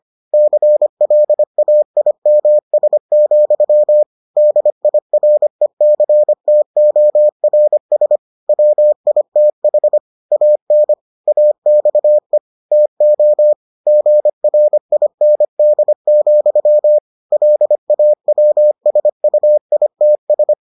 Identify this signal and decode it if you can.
Morse code, heard as a keyed tone, about 25 wpm, 600 Hz. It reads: CLAIMS, DIRECTORS WITH AN AXE TO GRIND, LAWSUITS